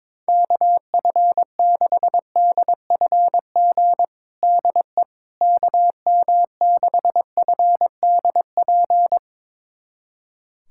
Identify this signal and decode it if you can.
Morse code, heard as a keyed tone, about 22 wpm, 700 Hz. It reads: KF6DFG DE KM6FDP
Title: USA Callsign DE Callsign - 6 Characters - 22wpm